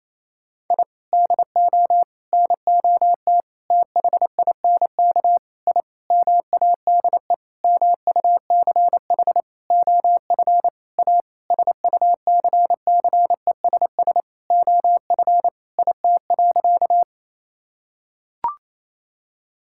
Morse code, heard as a keyed tone, about 28 wpm, 700 Hz. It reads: I DO NOT THINK I MADE MUCH OF A SUCCESS OF IT.
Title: Sentences from Top 1000 Words - Review Part 6 - 28wpm